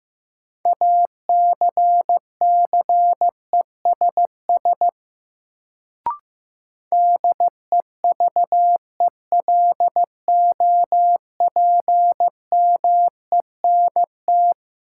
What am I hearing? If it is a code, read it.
Morse code, heard as a keyed tone, about 15 wpm, 700 Hz. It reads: ACCESS DEVELOPMENT
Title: Sentences from Top 1000 Words - Review Part 4 - New Word Drill - 15wpm